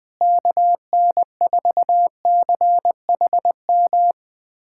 Morse code, heard as a keyed tone, about 21 wpm, 700 Hz. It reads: KN4CHM
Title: USA Callsign DE Callsign - 6 Characters - 20wpm